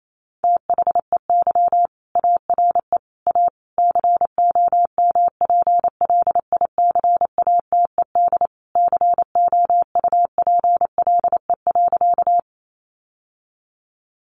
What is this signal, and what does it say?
Morse code, heard as a keyed tone, about 28 wpm, 700 Hz. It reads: THEY ARE A COMPLICATED COUPLE.
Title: Sentences from Top 1200 Words 28wpm